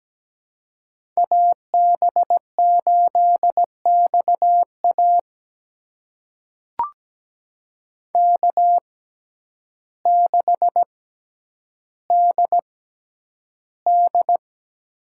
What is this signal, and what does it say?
Morse code, heard as a keyed tone, about 17 wpm, spaced as for 9 wpm, 700 Hz. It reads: AB8XA K6DD
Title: USA Call Signs - 6x Character Spacing - 17wpm